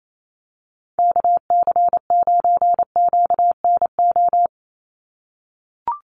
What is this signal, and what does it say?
Morse code, heard as a keyed tone, about 28 wpm, 700 Hz. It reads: KC9QNO
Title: USA Call Signs 28wpm